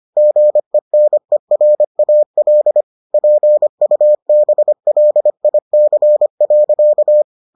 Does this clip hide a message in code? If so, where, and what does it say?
Yes: Morse code from 0.2 to 7.2 s, heard as a keyed tone, about 25 wpm, 600 Hz. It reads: GENERAL PUBLIC.